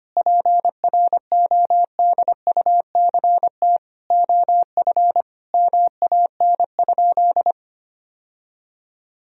Morse code, heard as a keyed tone, about 25 wpm, 700 Hz. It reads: PRODUCT OF MAN?
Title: Sentences from Top 1000 Words - Review Part 2 - 25wpm